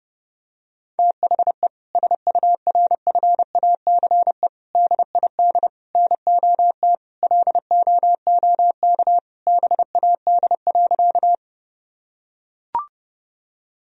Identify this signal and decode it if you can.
Morse code, heard as a keyed tone, about 32 wpm, 700 Hz. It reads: THE SURFACE DID NOT LOOK BAD.
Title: Sentences from Top 1000 Words - Review Part 3 - 30wpm